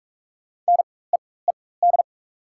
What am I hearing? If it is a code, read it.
Morse code, heard as a keyed tone, about 42 wpm, spaced as for 21 wpm, 700 Hz. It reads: NEED
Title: 4x Character Spacing - Top 500 Words - 45wpm